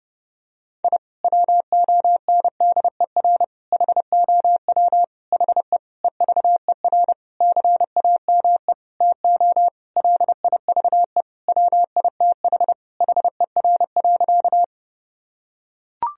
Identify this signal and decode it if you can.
Morse code, heard as a keyed tone, about 30 wpm, 700 Hz. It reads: I WONDER HOW HE EVER CAME TO LIVE WITH HER.
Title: Sentences from Top 1000 Words - Review Part 3 - 30wpm